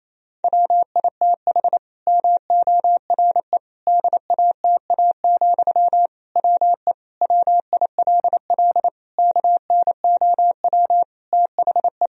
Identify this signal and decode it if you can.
Morse code, heard as a keyed tone, about 28 wpm, 700 Hz. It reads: WITH MORE DATA, WE WILL KNOW THE
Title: Sentences from Top 1000 Words - Review Part 3 - 28wpm